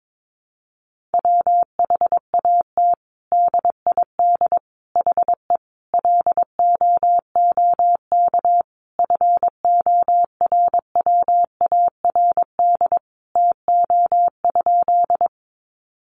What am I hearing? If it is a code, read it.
Morse code, heard as a keyed tone, about 22 wpm, 700 Hz. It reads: WHAT DID HE LOOK FORWARD TO?
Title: Sentences from Top 1300 Words 22wpm